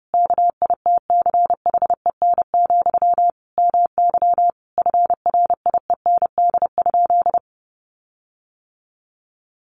Morse code, heard as a keyed tone, about 30 wpm, 700 Hz. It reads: KITCHEN, MY FRIEND?
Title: Sentences from Top 1400 Words 30wpm